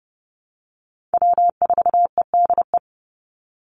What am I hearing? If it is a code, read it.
Morse code, heard as a keyed tone, about 30 wpm, 700 Hz. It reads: W4EDE
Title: Call Signs - 30wpm